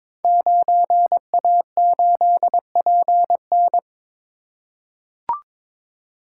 Morse code, heard as a keyed tone, about 22 wpm, 700 Hz. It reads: 9A8PN
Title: Code Groups - 5 Letter:Numbers - 22wpm